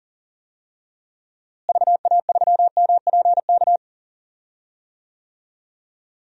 Morse code, heard as a keyed tone, about 40 wpm, 700 Hz. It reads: VA3MPK
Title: DXCC-100 Call Signs - 6 Characters - Repeated 2x - 40wpm